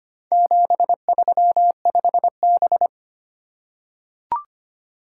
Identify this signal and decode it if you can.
Morse code, heard as a keyed tone, about 25 wpm, 700 Hz. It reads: 735B
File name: practice / 25wpm / Four Letter-Number Combinations - 25wpm